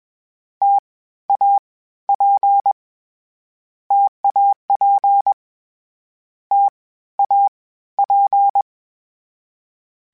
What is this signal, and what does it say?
Morse code, heard as a keyed tone, about 22 wpm, 800 Hz. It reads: T A P TAP T A P
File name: Variable Spacing / Variable Spacing-mp3s / T AOEs @20